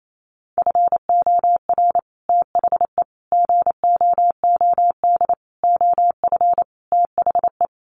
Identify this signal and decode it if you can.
Morse code, heard as a keyed tone, about 28 wpm, 700 Hz. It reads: FOR THE GOOD OF THE
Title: Sets of 5 Words Encore2 28wpm